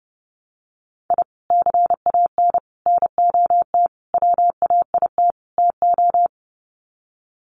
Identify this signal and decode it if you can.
Morse code, heard as a keyed tone, about 30 wpm, 700 Hz. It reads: I CAN NOT WAIT TO
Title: Sets of 5 Words Encore 30wpm